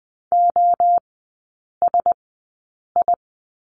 Morse code, heard as a keyed tone, about 20 wpm, spaced as for 8 wpm, 700 Hz. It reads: OSI
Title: 074 Lesson 13e - Letter L - Three Characters 20wpm